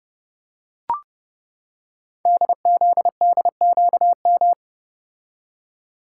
Morse code, heard as a keyed tone, about 30 wpm, 700 Hz. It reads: DZDQM
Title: Code Groups - 5 Letters - 30wpm